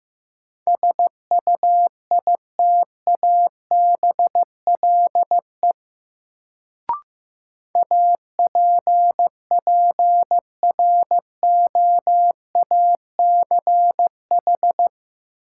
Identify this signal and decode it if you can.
Morse code, heard as a keyed tone, about 15 wpm, 700 Hz. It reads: SUITABLE APPROACH
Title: Sentences from Top 1400 Words - New Word Drill Alt - 15wpm